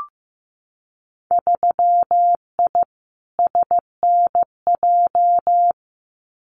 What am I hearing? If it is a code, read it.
Morse code, heard as a keyed tone, about 15 wpm, 700 Hz. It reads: 3I SNJ
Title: Winter Field Day Exchanges Encore 15wpm